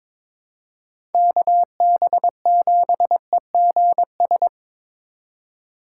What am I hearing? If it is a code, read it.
Morse code, heard as a keyed tone, about 22 wpm, 700 Hz. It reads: KB7EGS
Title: USA Call Signs - 6 Characters - 3x Character Spacing - 22wpm